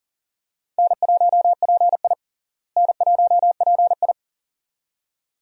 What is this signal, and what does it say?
Morse code, heard as a keyed tone, about 40 wpm, 700 Hz. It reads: N1PI N1PI
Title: USA Call Signs - 4 Characters - Repeated 2x - 40wpm